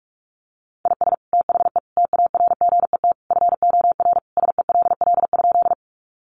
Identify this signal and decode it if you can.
Morse code, heard as a keyed tone, about 44 wpm, 700 Hz. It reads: IS THE TARGET FOR SELL?
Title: Sentences from Top 1200 Words 45wpm